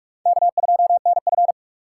Morse code, heard as a keyed tone, about 61 wpm, spaced as for 53 wpm, 700 Hz. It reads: K2NF
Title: Speed Racing - USA Call Signs 45wpm